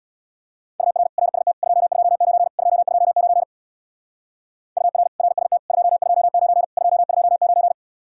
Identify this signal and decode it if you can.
Morse code, heard as a keyed tone, about 60 wpm, 700 Hz. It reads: UR RST 229 229 UR RST 229 229
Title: Speed-Racing Common QSO Phrases Encore - 50wpm